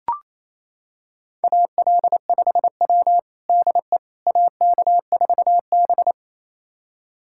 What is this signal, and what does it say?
Morse code, heard as a keyed tone, about 28 wpm, 700 Hz. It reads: AL5W DE AK4B
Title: USA Callsign DE Callsign - 4 Characters - 28wpm